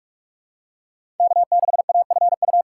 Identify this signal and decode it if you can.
Morse code, heard as a keyed tone, about 46 wpm, 700 Hz. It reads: K6AFU
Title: USA Call Signs - 2x Character Spacing - 45wpm